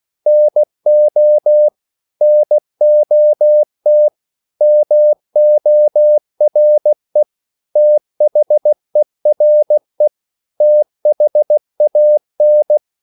Morse code, heard as a keyed tone, about 16 wpm, 600 Hz. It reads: NO NOT MORE THERE THAN
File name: CW / Practice / 500MostCommonEnglishWords16